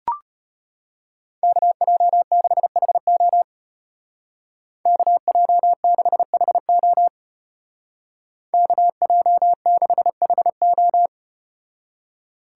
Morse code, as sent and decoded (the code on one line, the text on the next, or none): -.- .--- -.... .... --- / -.- .--- -.... .... --- / -.- .--- -.... .... ---
KJ6HO KJ6HO KJ6HO